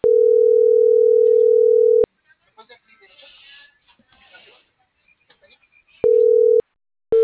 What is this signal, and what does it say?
Ringback tone (North America), 440 + 480 Hz.